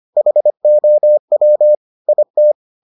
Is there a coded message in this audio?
Yes: Morse code from 0.2 to 2.5 s, heard as a keyed tone, about 25 wpm, 600 Hz. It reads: HOW IT